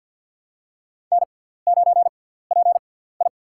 Morse code, heard as a keyed tone, about 49 wpm, spaced as for 17 wpm, 700 Hz. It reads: N9PI
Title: USA Call Signs - 6x Character Spacing - 50wpm